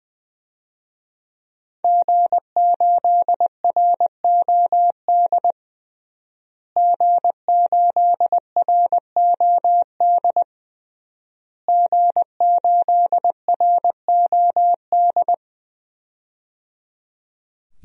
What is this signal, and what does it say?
Morse code, heard as a keyed tone, about 20 wpm, 700 Hz. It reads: G8ROD G8ROD G8ROD
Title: DXCC-100 Call Signs - 5 Characters - Repeated 3x - 20wpm